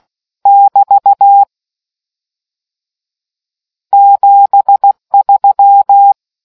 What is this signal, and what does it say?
Morse code, heard as a keyed tone, about 16 wpm, 800 Hz. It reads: <BT> 73